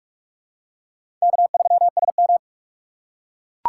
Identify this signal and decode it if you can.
Morse code, heard as a keyed tone, about 44 wpm, 700 Hz. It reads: K3SM
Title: USA Call Signs - 4 Characters - 3x Character Spacing - 45wpm